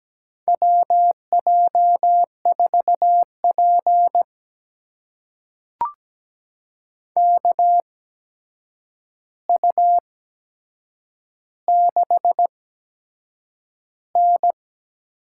Morse code, heard as a keyed tone, about 17 wpm, spaced as for 7 wpm, 700 Hz. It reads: WJ4P KU6N
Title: USA Call Signs - 4 Characters - 8x Character Spacing - 17wpm